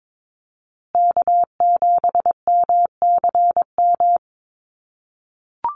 Morse code, heard as a keyed tone, about 22 wpm, 700 Hz. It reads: K7MCM
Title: USA Call Signs Repeated 3x 22wpm